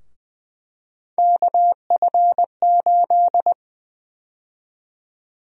Morse code, heard as a keyed tone, about 20 wpm, 700 Hz. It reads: KF8
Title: USA Call Sign Prefixes - 20wpm